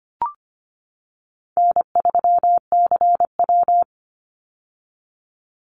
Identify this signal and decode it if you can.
Morse code, heard as a keyed tone, about 25 wpm, 700 Hz. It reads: N3CW
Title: Easy Call Signs - 25wpm